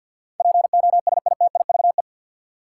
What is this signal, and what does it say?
Morse code, heard as a keyed tone, about 52 wpm, 700 Hz. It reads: POSITIVE
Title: Sentences from Top 1400 Words - New Word Drill Alt - 50wpm